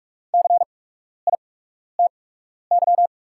Morse code, heard as a keyed tone, about 44 wpm, spaced as for 12 wpm, 690 Hz. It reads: CITY
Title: 8x Character Spacing - Top 300 Words - 45wpm